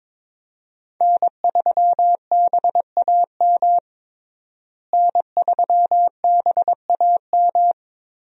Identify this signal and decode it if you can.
Morse code, heard as a keyed tone, about 22 wpm, 700 Hz. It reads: N3BAM N3BAM
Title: USA Call Signs - 5 Characters - Repeated 2x - 22wpm